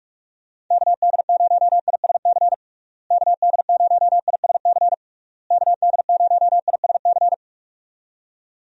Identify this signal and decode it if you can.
Morse code, heard as a keyed tone, about 44 wpm, 700 Hz. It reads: KD0ISC KD0ISC KD0ISC
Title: USA Call Signs - 6 Characters - Repeated 3x - 45wpm